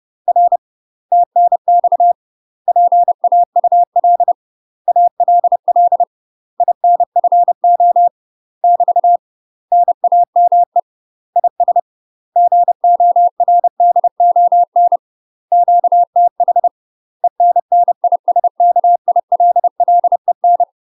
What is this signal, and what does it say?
Morse code, heard as a keyed tone, about 30 wpm, 700 Hz. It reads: R TNX PAUL ALL INFO <BT> NAME IS GORDON QTH ENNISKILLEN
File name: UK MASTER QSOs TRAINING / 30WPM / UK MASTER QSOs 0020